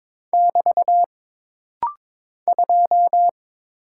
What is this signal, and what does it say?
Morse code, heard as a keyed tone, about 22 wpm, 700 Hz. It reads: <BT> 2
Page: This rapid-fire timing is geared to push your ICR (Instant Character Recognition) skills.